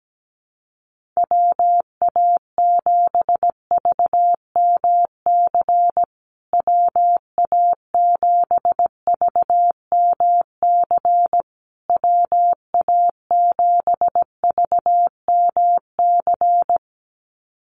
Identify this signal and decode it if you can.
Morse code, heard as a keyed tone, about 17 wpm, 700 Hz. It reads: WA7VMC WA7VMC WA7VMC
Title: USA Call Signs Repeated 3x 17wpm